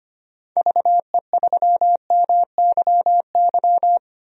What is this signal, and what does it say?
Morse code, heard as a keyed tone, about 25 wpm, 700 Hz. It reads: VE3MYY
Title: DXCC-100 Call Signs - 6 Characters - 25wpm